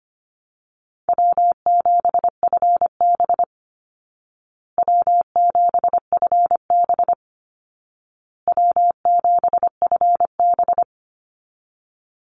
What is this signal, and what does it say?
Morse code, heard as a keyed tone, about 25 wpm, 700 Hz. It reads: W7FB W7FB W7FB
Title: Easy Call Signs Repeated 3x - 25wpm